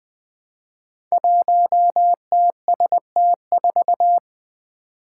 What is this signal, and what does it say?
Morse code, heard as a keyed tone, about 20 wpm, 700 Hz. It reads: 1TST4
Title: Five Challenging Letter-Number Combinations 20wpm